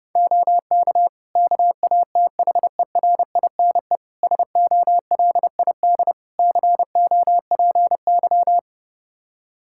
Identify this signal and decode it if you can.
Morse code, heard as a keyed tone, about 30 wpm, 700 Hz. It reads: OK KATHERINE SOLID COPY
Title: Practice QSO - Exchange 3 - More Info - 30wpm